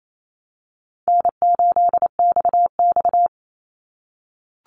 Morse code, heard as a keyed tone, about 28 wpm, 700 Hz. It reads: N8XX
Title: Easy Call Signs - 28wpm